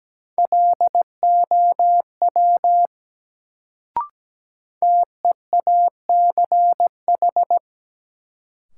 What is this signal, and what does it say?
Morse code, heard as a keyed tone, about 17 wpm, 700 Hz. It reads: LOW TEACH
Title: Sentences from Top 500 Words - New Word Drill - 17wpm